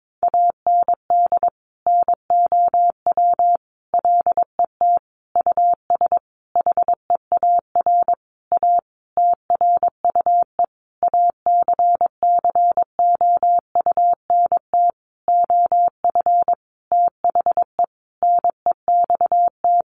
Morse code, heard as a keyed tone, about 22 wpm, 700 Hz. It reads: AND NOW LET US HEAR A TRUE ACCOUNT OF THE NEXT
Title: Sentences from Top 900 Words 22wpm